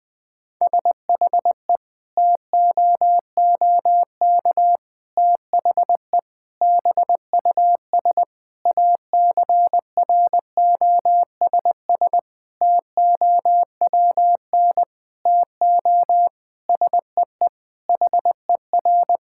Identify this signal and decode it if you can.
Morse code, heard as a keyed tone, about 20 wpm, 700 Hz. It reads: SHE TOOK THE BUS ACROSS TOWN TO SEE HER
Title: Sentences from Top 1000 Words - Review Part 6 - 20wpm